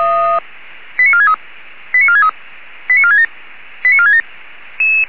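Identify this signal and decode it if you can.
Five-tone selective call (ZVEI), heard as quick runs of five tones, tones of about 70 ms. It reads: 89463 89463 89467 89467
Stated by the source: ZVEI five-tone call 89330 received by SDR.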